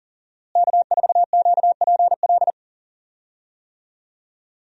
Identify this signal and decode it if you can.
Morse code, heard as a keyed tone, about 37 wpm, 700 Hz. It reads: K4QPL
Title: USA Call Signs - 5 Characters - Repeated 2x - 40wpm